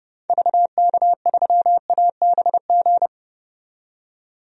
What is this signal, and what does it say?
Morse code, heard as a keyed tone, about 30 wpm, 700 Hz. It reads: VK3ABG
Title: DXCC-100 Call Signs - 6 Characters - 30wpm